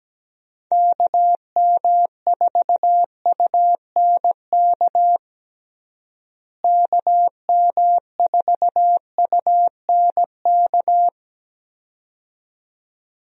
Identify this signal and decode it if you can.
Morse code, heard as a keyed tone, about 17 wpm, 700 Hz. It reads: KM4UNK KM4UNK
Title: USA Call Signs - 6 Characters - Repeated 2x - 17wpm